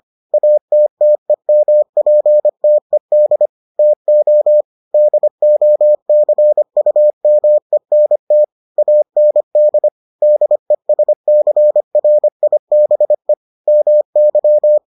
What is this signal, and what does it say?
Morse code, heard as a keyed tone, about 25 wpm, 600 Hz. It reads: ATTEMPTED TO DOCUMENT AND DESCRIBE MY